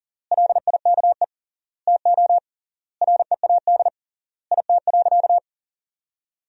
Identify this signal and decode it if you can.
Morse code, heard as a keyed tone, about 40 wpm, 700 Hz. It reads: LIKE TO READ IT.
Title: 3x Word-Spacing - Sentences from Top 500 Words - 40wpm